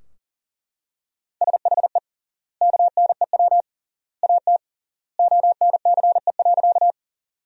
Morse code, heard as a keyed tone, about 40 wpm, 700 Hz. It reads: SHE KNEW AT ONCE.
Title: 3x Word-Spacing - Sentences from Top 400 Words - 40wpm